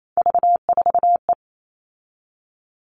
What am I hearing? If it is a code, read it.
Morse code, heard as a keyed tone, about 28 wpm, 700 Hz. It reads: V4E
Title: All Cut Numbers - 3-Digit 28wpm